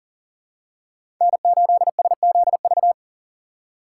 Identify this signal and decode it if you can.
Morse code, heard as a keyed tone, about 40 wpm, 700 Hz. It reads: N8SZV
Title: USA Call Signs - 5 Characters - 4x Character Spacing - 40wpm